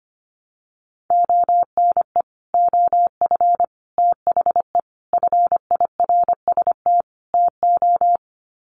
Morse code, heard as a keyed tone, about 25 wpm, 700 Hz. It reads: ONE OF THE FIRST TO
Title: Sets of 5 Words Encore 25wpm